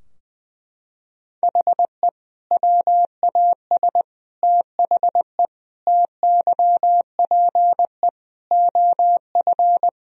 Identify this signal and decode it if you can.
Morse code, heard as a keyed tone, about 20 wpm, 700 Hz. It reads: HE WAS THE TYPE OF
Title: Sentences from Top 1000 Words - Review Part 3 - 20wpm